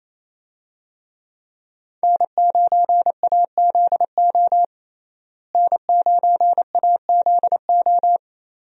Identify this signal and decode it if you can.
Morse code, heard as a keyed tone, about 28 wpm, 700 Hz. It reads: N9AZO N9AZO
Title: USA Call Signs - 5 Characters - Repeated 2x - 28wpm